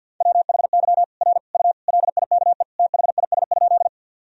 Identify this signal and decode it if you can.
Morse code, heard as a keyed tone, about 45 wpm, 700 Hz. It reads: WHY R U LIKE THIS?
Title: Sentences from Top 200 Words and Abbreviations 50wpm